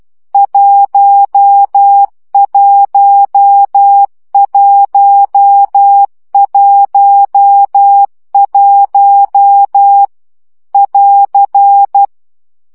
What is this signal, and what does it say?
Morse code, heard as a keyed tone, about 12 wpm, 800 Hz. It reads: 11111 <AR>
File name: CW gyakorlas 1 10 baud